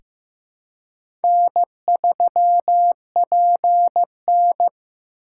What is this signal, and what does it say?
Morse code, heard as a keyed tone, about 15 wpm, 700 Hz. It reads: N3PN
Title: USA Call Signs - 4x Character Spacing - 15wpm